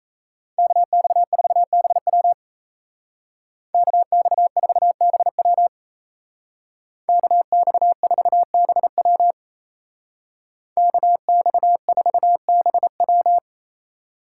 Morse code, heard as a keyed tone, about 35 wpm, 700 Hz. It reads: KX4BW KX4BW KX4BW KX4BW
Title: Speed Racing - USA Call Signs 28wpm